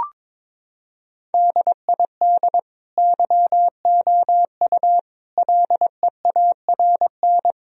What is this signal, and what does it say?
Morse code, heard as a keyed tone, about 22 wpm, 700 Hz. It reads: DID YOU LEARN
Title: Sentences from Top 1000 Words - Review Part 4 - 22wpm